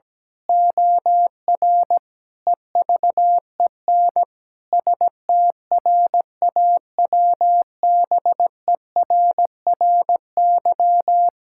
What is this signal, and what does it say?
Morse code, heard as a keyed tone, about 17 wpm, 700 Hz. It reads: OR EVEN STRAWBERRY
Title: Pangram Challenge v2 - 17wpm